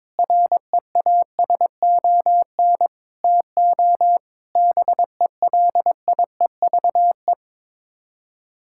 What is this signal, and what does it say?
Morse code, heard as a keyed tone, about 22 wpm, 700 Hz. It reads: REASON TO BELIEVE
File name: practice / 22wpm / Sets of 6 Words - 22wpm